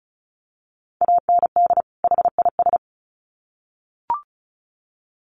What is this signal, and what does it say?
Morse code, heard as a keyed tone, about 34 wpm, 700 Hz. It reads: AND HIS
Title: Sets of 2 Words 35wpm